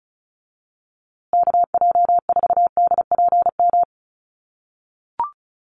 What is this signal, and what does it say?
Morse code, heard as a keyed tone, about 34 wpm, 700 Hz. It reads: KJ4DPM
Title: USA Call Signs Repeated 2x 35wpm